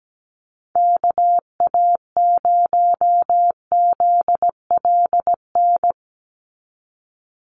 Morse code, heard as a keyed tone, about 17 wpm, 700 Hz. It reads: KA0ZLN
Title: USA Call Signs 17wpm